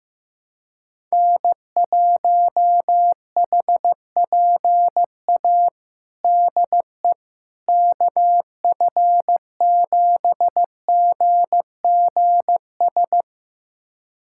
Morse code, heard as a keyed tone, about 16 wpm, 700 Hz. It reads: N1HPA DE KF7GGS
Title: USA Callsign DE Callsign 15wpm